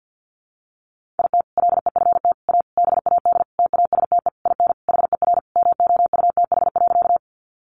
Morse code, heard as a keyed tone, about 50 wpm, 700 Hz. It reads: IT LEFT A BAD TASTE IN HER MOUTH.
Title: Sentences from Top 1500 Words 50wpm